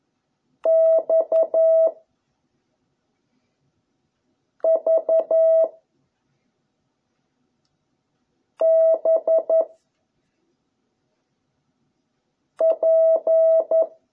Morse code, heard as a keyed tone, about 11 wpm, 660 Hz. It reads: X V B P